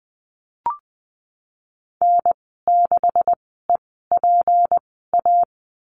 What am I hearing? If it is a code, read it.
Morse code, heard as a keyed tone, about 20 wpm, spaced as for 15 wpm, 700 Hz. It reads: N6EPA
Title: 167 Lesson 27g - Number 7 - Call Signs 20wpm